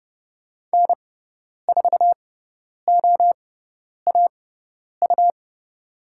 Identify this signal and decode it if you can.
Morse code, heard as a keyed tone, about 30 wpm, spaced as for 10 wpm, 700 Hz. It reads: N4OAU
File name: Lesson 17g - Letter u - Call Signs 30wpm